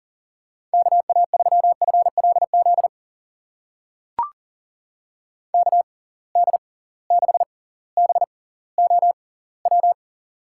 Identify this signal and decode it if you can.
Morse code, heard as a keyed tone, about 39 wpm, spaced as for 20 wpm, 700 Hz. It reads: KA3FLZ KD6BOW
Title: USA Call Signs - 6 Characters - 6x Character Spacing - 40wpm